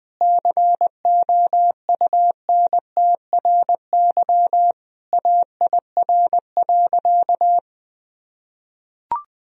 Morse code, heard as a keyed tone, about 20 wpm, 700 Hz. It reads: COUNTRY AIR.
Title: Sentences from Top 1000 Words - Review Part 1 - 20wpm